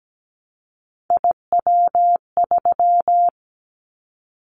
Morse code, heard as a keyed tone, about 17 wpm, 700 Hz. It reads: IW3
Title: Callsign Prefixes for Top 25 DXCC - Repeated 3x - 17wpm